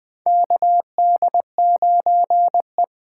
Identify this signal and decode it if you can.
Morse code, heard as a keyed tone, about 20 wpm, 700 Hz. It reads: KD9E
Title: USA Call Signs - 6 Characters - 8x Character Spacing - 20wpm